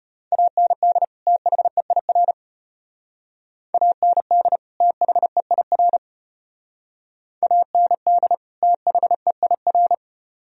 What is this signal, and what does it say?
Morse code, heard as a keyed tone, about 34 wpm, 700 Hz. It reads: AND THEIR AND THEIR AND THEIR